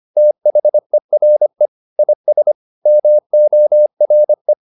Morse code, heard as a keyed tone, about 25 wpm, 600 Hz. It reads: THERE IS MORE